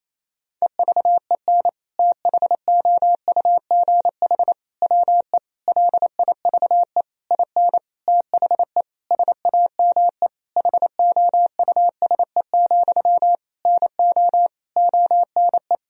Morse code, heard as a keyed tone, about 27 wpm, 700 Hz. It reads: EVEN THOUGH WE LIVE IN THE SAME HOUSE, NO ONE
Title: Sentences from Top 1000 Words - Review Part 2 - 28wpm